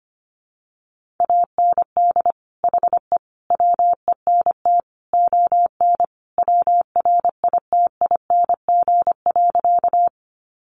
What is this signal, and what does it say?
Morse code, heard as a keyed tone, about 25 wpm, 700 Hz. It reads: AND HE WENT ON WRITING.
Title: Sentences from Top 800 Words 25wpm